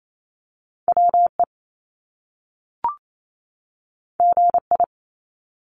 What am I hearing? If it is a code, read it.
Morse code, heard as a keyed tone, about 28 wpm, 700 Hz. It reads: WE GI
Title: Top Two Letter Combinations in Words 28wpm